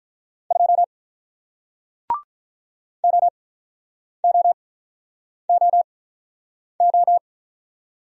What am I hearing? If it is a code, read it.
Morse code, heard as a keyed tone, about 44 wpm, spaced as for 9 wpm, 690 Hz. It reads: 2 OOOO